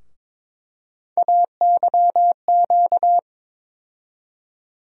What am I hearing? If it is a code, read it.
Morse code, heard as a keyed tone, about 22 wpm, 700 Hz. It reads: AYQ
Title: USA Call Sign Postfixes - 3 Letters 22wpm